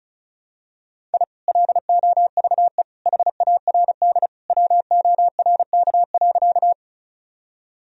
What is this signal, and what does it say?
Morse code, heard as a keyed tone, about 36 wpm, 700 Hz. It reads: I LOVE HARD WORK.
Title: Sentences from Top 1000 Words - Review Part 1 - 35wpm